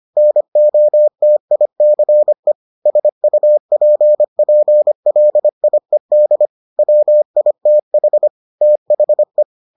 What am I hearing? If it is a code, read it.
Morse code, heard as a keyed tone, about 25 wpm, 600 Hz. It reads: NOTICE SUPPLIED WITH THE